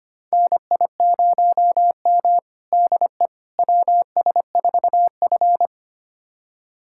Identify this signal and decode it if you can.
Morse code, heard as a keyed tone, about 25 wpm, 700 Hz. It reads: NI0M DE WS4F
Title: USA Callsign DE Callsign - 4 Characters - 25wpm